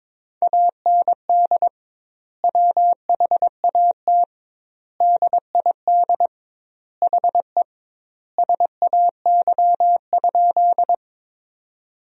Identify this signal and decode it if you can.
Morse code, heard as a keyed tone, about 22 wpm, 700 Hz. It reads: AND WHAT DID HE SAY?
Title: 2x Word-Spacing - Sentences from Top 200 Words - 22wpm